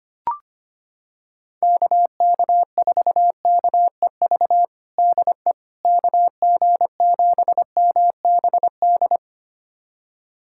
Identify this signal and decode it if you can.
Morse code, heard as a keyed tone, about 25 wpm, 700 Hz. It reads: KK4KEV DE KG7MBD
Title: USA Callsign DE Callsign - 6 Characters - 25wpm